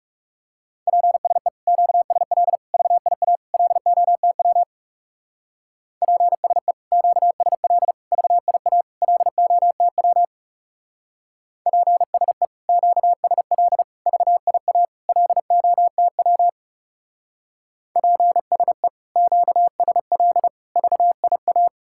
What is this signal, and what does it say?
Morse code, heard as a keyed tone, about 36 wpm, 700 Hz. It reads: PSE QSL VIA LOTW PSE QSL VIA LOTW PSE QSL VIA LOTW PSE QSL VIA
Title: Speed-Racing Common QSO Phrases - 30wpm